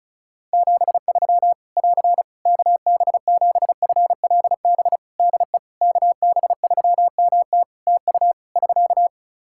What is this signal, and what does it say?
Morse code, heard as a keyed tone, about 36 wpm, 700 Hz. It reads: 73 <AR> KB7FLB DE KB3MT TU <SK>
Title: Practice QSO - Exchange 4 - Ending - 35wpm